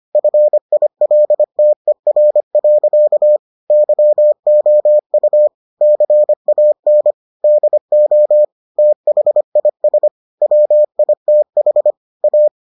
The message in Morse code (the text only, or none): FILTER. YOU CAN DO THIS WITH A